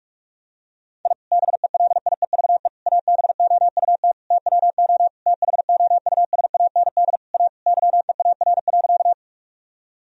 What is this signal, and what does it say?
Morse code, heard as a keyed tone, about 41 wpm, 700 Hz. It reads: I BELIEVE ABOUT TWO THOUSAND A YEAR.
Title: Sentences from Top 1000 Words - Review Part 3 - 45wpm